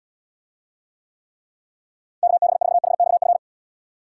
Morse code, heard as a keyed tone, about 78 wpm, 700 Hz. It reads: KB4HXF